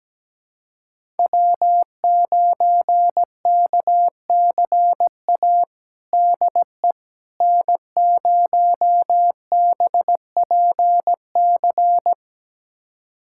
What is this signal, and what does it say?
Morse code, heard as a keyed tone, about 17 wpm, 700 Hz. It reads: W9KCA DE N0BPC
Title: USA Callsign DE Callsign - 5 Characters - 17wpm